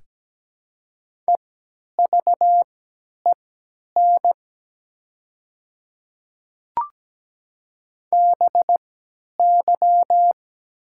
Morse code, heard as a keyed tone, about 17 wpm, spaced as for 10 wpm, 700 Hz. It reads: EVEN BY
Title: 3x Character Spacing - Top 200 Words - 17wpm